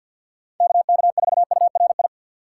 Morse code, heard as a keyed tone, about 47 wpm, 700 Hz. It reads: KK4URI
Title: USA Call Signs - 6x Character Spacing - 50wpm